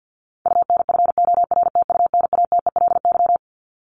Morse code, heard as a keyed tone, about 50 wpm, 700 Hz. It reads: UNFORTUNATELY
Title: LY Suffix - Top 100 Words with Base Words 50wpm